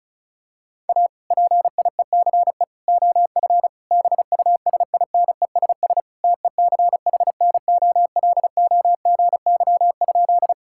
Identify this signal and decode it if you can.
Morse code, heard as a keyed tone, about 34 wpm, 690 Hz. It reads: A PIECE OF BUSINESS TECHNOLOGY?
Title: Sentences from Top 1000 Words - Review Part 5 - 35wpm